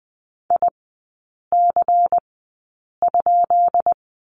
Morse code, heard as a keyed tone, about 20 wpm, spaced as for 8 wpm, 700 Hz. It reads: IC?
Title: 145 Lesson 24e - Letter y - Three Characters 20wpm